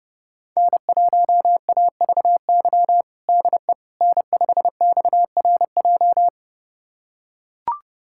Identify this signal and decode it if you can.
Morse code, heard as a keyed tone, about 32 wpm, 700 Hz. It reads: N1AVY DE N5XRJ
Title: USA Callsign DE Callsign - 5 Characters - 30wpm